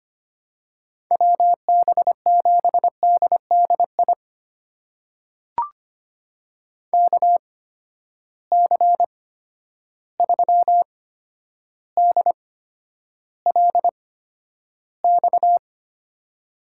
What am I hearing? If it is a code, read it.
Morse code, heard as a keyed tone, about 25 wpm, spaced as for 11 wpm, 700 Hz. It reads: WB7DDI KC3DLX
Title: USA Call Signs - 6 Characters - 8x Character Spacing - 25wpm